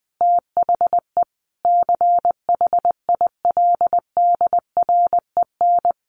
Morse code, heard as a keyed tone, about 20 wpm, 700 Hz. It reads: THE CHILDREN
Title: Sentences from Top 1300 Words 20wpm